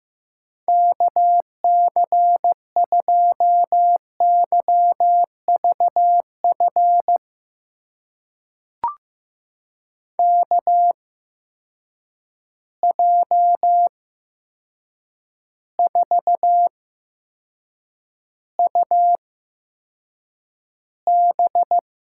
Morse code, heard as a keyed tone, about 16 wpm, spaced as for 7 wpm, 700 Hz. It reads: KC2YVF KJ4UB
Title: USA Call Signs - 6 Characters - 8x Character Spacing - 15wpm